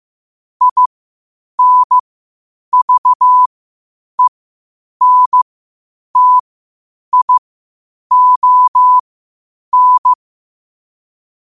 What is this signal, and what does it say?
Morse code, heard as a keyed tone, about 15 wpm, spaced as for 8 wpm, 1000 Hz. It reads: INVENTION
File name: qotdfiles / 8wpm